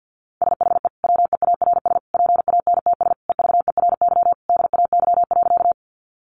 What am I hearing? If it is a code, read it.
Morse code, heard as a keyed tone, about 50 wpm, 700 Hz. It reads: SHE WEARS PANTS EVERY DAY.
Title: Additional Easy Phrases 50wpm